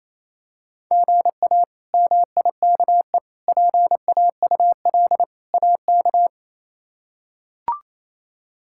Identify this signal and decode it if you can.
Morse code, heard as a keyed tone, about 28 wpm, 700 Hz. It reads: GA MIKE PAUL AK
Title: K1USN SST QSO Practice - 28wpm